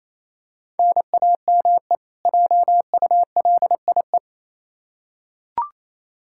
Morse code, heard as a keyed tone, about 28 wpm, 700 Hz. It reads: NAME JULIE
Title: K1USN SST QSO Practice 28wpm